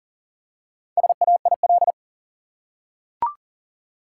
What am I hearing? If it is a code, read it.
Morse code, heard as a keyed tone, about 40 wpm, 700 Hz. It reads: SAIL
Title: Four Letter Words - 40wpm